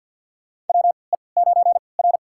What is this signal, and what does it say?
Morse code, heard as a keyed tone, about 52 wpm, spaced as for 29 wpm, 700 Hz. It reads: WE9R
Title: USA Call Signs - 4 Characters - 3x Character Spacing - 50wpm